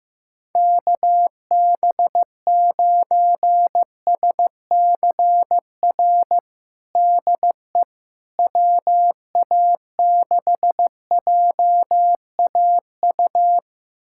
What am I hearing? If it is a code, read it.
Morse code, heard as a keyed tone, about 15 wpm, 700 Hz. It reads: KB9SCR DE WA6JAU
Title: USA Callsign DE Callsign - 6 Characters - 15wpm